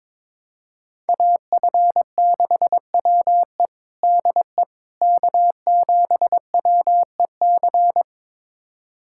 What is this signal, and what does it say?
Morse code, heard as a keyed tone, about 22 wpm, 700 Hz. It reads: AF6WE DE K7WEC
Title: USA Callsign DE Callsign - 5 Characters - 22wpm